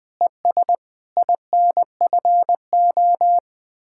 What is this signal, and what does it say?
Morse code, heard as a keyed tone, about 20 wpm, 700 Hz. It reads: ES INFO